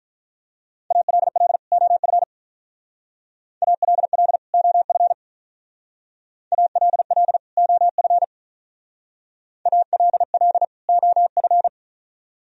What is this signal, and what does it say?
Morse code, heard as a keyed tone, about 43 wpm, 700 Hz. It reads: ALL OF ALL OF ALL OF ALL OF
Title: Speed-Racing - Sets of 2 Words 35wpm